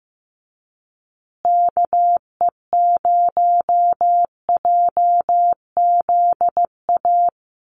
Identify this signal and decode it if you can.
Morse code, heard as a keyed tone, about 15 wpm, 700 Hz. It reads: KE0JZA
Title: USA Call Signs Repeated 2x 15wpm